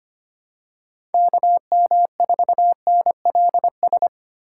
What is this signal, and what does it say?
Morse code, heard as a keyed tone, about 25 wpm, 700 Hz. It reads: KM4NLS
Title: USA Call Signs - 6 Characters - 3x Character Spacing - 25wpm